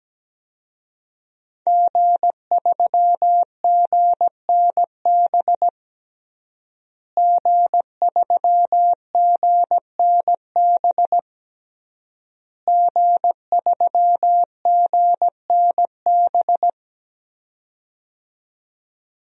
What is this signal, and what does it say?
Morse code, heard as a keyed tone, about 17 wpm, 700 Hz. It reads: G3GNB G3GNB G3GNB
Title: DXCC-100 Call Signs - 5 Characters - Repeated 3x - 17wpm